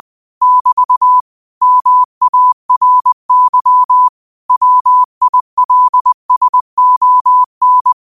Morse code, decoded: <BT> MARY WILSON

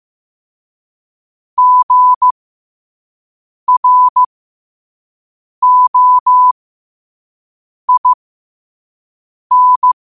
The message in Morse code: GROIN

